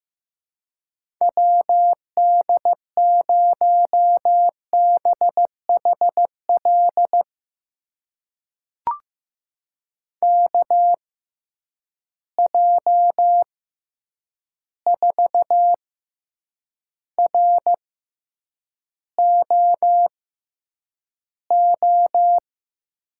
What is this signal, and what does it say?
Morse code, heard as a keyed tone, about 15 wpm, spaced as for 8 wpm, 700 Hz. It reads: WD0BHL KJ4ROO
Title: USA Call Signs - 6 Characters - 6x Character Spacing - 15wpm